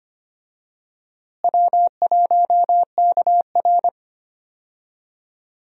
Morse code, heard as a keyed tone, about 25 wpm, 700 Hz. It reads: W1KR
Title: USA Call Signs - 4 Characters - Repeated 3x - 25wpm